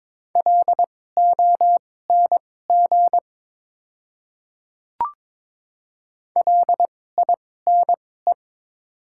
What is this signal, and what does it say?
Morse code, heard as a keyed tone, about 22 wpm, spaced as for 16 wpm, 700 Hz. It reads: LONG LINE
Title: 2x Character Spacing - Top 200 Words - 22wpm